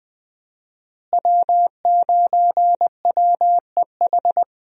Morse code, heard as a keyed tone, about 20 wpm, 700 Hz. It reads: W9WEH
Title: USA Call Signs - 5 Characters - Repeated 2x - 20wpm